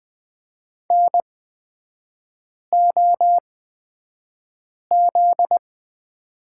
Morse code, heard as a keyed tone, about 20 wpm, spaced as for 5 wpm, 700 Hz. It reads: NOZ
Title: 250 Lesson 40e - BK - Three Characters 20wpm